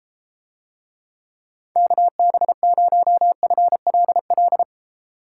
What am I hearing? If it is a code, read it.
Morse code, heard as a keyed tone, about 31 wpm, 700 Hz. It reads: KB0FLL